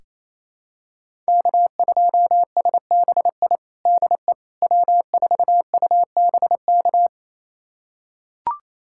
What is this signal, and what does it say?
Morse code, heard as a keyed tone, about 27 wpm, 700 Hz. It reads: K2SBI DE W4UBK
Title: USA Callsign DE Callsign - 5 Characters - 28wpm